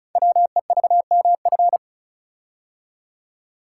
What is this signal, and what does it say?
Morse code, heard as a keyed tone, about 36 wpm, 700 Hz. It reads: WEVMF
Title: Code Groups - 5 Letters - 35wpm